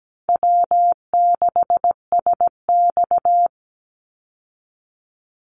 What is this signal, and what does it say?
Morse code, heard as a keyed tone, about 17 wpm, 700 Hz. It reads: W6SX
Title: Easy Call Signs Repeated 3x - 17wpm